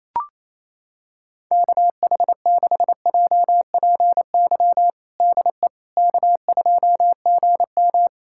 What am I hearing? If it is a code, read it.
Morse code, heard as a keyed tone, about 28 wpm, 700 Hz. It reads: KH6JPY DE K2GM
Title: USA Callsign DE Callsign 28wpm